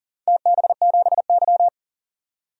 Morse code, heard as a keyed tone, about 37 wpm, 700 Hz. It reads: TB7Y
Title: DXCC-100 Call Signs - 4 Characters - 40wpm